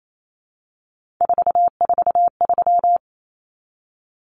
Morse code, heard as a keyed tone, about 28 wpm, 700 Hz. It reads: 443
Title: Common Cut Numbers - 3-Digit 28wpm